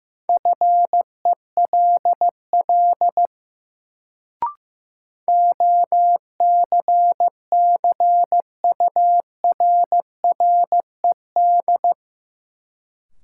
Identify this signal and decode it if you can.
Morse code, heard as a keyed tone, about 15 wpm, 700 Hz. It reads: FELL OCCURRED
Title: Sets of 5 Words Encore - New Word Drill - 15wpm